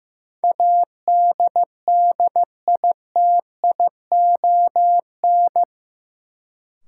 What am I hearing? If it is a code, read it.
Morse code, heard as a keyed tone, about 15 wpm, 700 Hz. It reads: ADDITION
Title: Sentences from Top 1500 Words - New Word Drill Alt - 15wpm